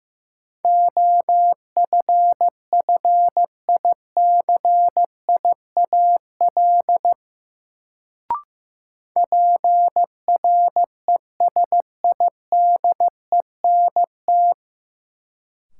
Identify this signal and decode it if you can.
Morse code, heard as a keyed tone, about 15 wpm, 700 Hz. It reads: OFFICIAL PRESIDENT
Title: Sentences from Top 1000 Words - New Word Drill - 15wpm